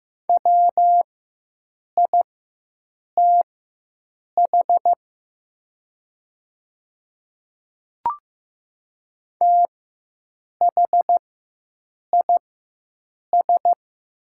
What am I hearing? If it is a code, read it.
Morse code, heard as a keyed tone, about 15 wpm, spaced as for 7 wpm, 700 Hz. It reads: WITH THIS
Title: 4x Character Spacing - Top 100 Words - 15wpm